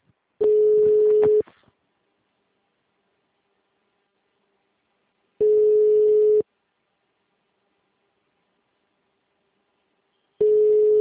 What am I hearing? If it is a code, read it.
Ringback tone (Europe, 425 Hz).